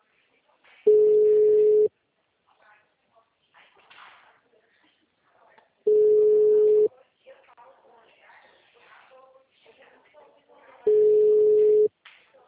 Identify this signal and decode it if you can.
Ringback tone (Europe, 425 Hz), with speech.
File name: out- -2041- - - .5835